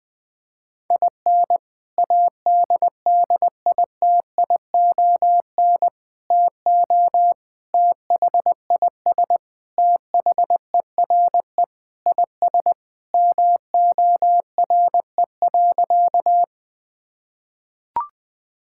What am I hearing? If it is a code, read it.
Morse code, heard as a keyed tone, about 20 wpm, 700 Hz. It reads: IN ADDITION TO THIS THERE IS MORE.
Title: Sentences with 3-Word Phrases - Spoken Beginning - 20wpm